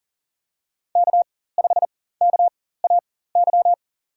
Morse code, heard as a keyed tone, about 39 wpm, spaced as for 18 wpm, 700 Hz. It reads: K5KAY
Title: USA Call Signs - 5 Characters - 4x Character Spacing - 40wpm